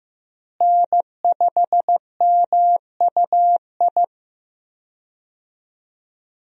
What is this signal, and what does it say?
Morse code, heard as a keyed tone, about 15 wpm, 700 Hz. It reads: N5MUI
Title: USA Call Signs - 5 Characters - Repeated 2x - 15wpm